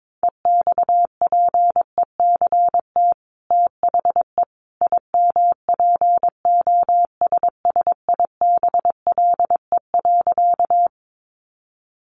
Morse code, heard as a keyed tone, about 22 wpm, 700 Hz. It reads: EXPECT THE IMPOSSIBLE.
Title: Sentences from Top 1100 Words 22wpm